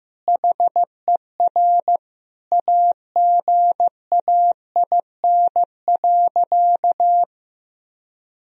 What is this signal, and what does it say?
Morse code, heard as a keyed tone, about 15 wpm, 700 Hz. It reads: HER AGAIN.
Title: Sentences from Top 1000 Words - Review Part 4 - 15wpm